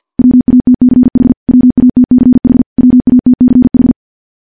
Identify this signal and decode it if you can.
Morse code, heard as a keyed tone, about 50 wpm, 250 Hz. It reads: WATCH WATCH WATCH
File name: morse-training-50WPM